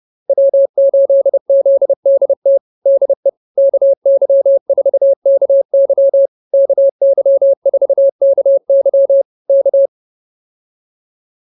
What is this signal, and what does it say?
Morse code, heard as a keyed tone, about 28 wpm, 550 Hz. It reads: W8ZDT DE KY4KY KY4KY K
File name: QSO108 30